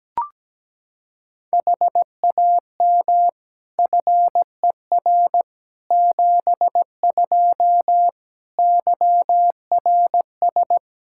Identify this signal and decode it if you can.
Morse code, heard as a keyed tone, about 17 wpm, 700 Hz. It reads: HAM FER 72 YRS
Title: Practice QSO - Exchange 3 - More Info - 17wpm